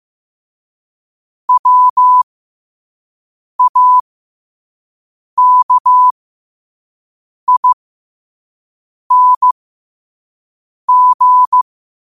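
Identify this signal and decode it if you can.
Morse code, heard as a keyed tone, about 15 wpm, spaced as for 5 wpm, 1000 Hz. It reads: WAKING
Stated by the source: Quote-of-the-day Morse code practice at 5 wpm.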